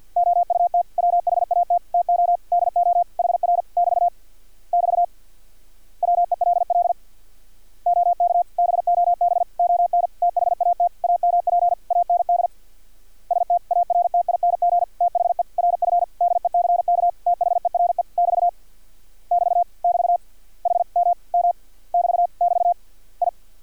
Morse code, heard as a keyed tone, about 50 wpm, 700 Hz. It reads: OUT WHAT TO DO HR <BT> <BT> WELL OK BOB ON THAT AMP AND STARTING THE RF DECK THERE <BT> <BT> <BT> HMM <BT> <BT> I
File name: archived / CRSnet